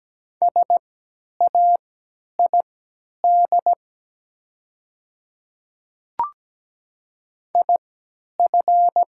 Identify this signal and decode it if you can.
Morse code, heard as a keyed tone, about 17 wpm, spaced as for 10 wpm, 700 Hz. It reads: SAID IF
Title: 3x Character Spacing - Top 100 Words - 17wpm